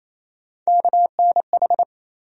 Morse code, heard as a keyed tone, about 27 wpm, 700 Hz. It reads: KNH
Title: USA Call Sign Postfixes - 3 Letters 28wpm